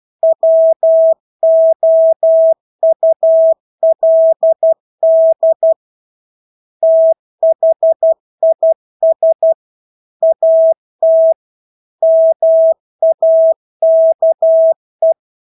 Morse code, heard as a keyed tone, about 12 wpm, 650 Hz. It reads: WOULD THIS AT MAKE
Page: Engelske ord 12 wpm